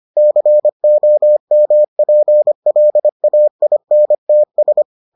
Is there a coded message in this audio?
Yes: Morse code from 0.2 to 4.8 s, heard as a keyed tone, about 25 wpm, 600 Hz. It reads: COMPLAINTS